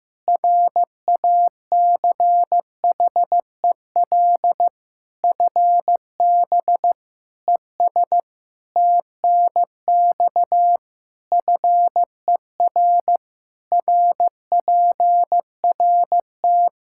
Morse code, heard as a keyed tone, about 16 wpm, 700 Hz. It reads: RACHEL FB ES TNX FER RPRT
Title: K1USN SST QSO Practice 15wpm